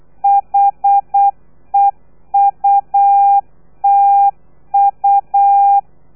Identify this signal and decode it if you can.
Morse code, heard as a keyed tone, about 8 wpm, 800 Hz. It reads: HEUTU
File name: telegrafia / i2tfj / Lez3 35